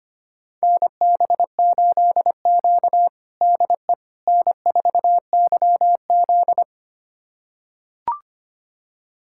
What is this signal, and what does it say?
Morse code, heard as a keyed tone, about 25 wpm, 700 Hz. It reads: NB8Q DE N4YZ
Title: USA Callsign DE Callsign - 4 Characters - 25wpm